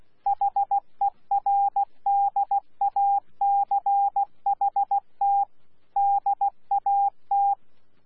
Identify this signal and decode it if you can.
Morse code, heard as a keyed tone, about 16 wpm, 800 Hz. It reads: HERDACHT DAT
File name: Z PI4VRZ / PI4VRZ 14-jun-2025